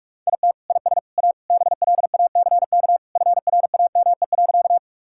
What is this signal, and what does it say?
Morse code, heard as a keyed tone, about 41 wpm, 700 Hz. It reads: IT IS A BLACK FRAME.
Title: Sentences from Top 1000 Words - Review Part 6 - 45wpm